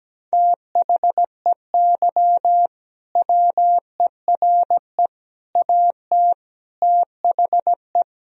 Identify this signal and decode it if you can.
Morse code, heard as a keyed tone, about 17 wpm, 700 Hz. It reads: THEY WERE AT THE
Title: Sentences from Top 1000 Words - Review Part 3 - 17wpm